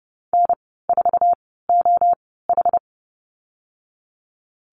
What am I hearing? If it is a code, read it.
Morse code, heard as a keyed tone, about 30 wpm, spaced as for 17 wpm, 700 Hz. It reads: N4OH
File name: Lesson 14g - Number 2 - Call Signs 30wpm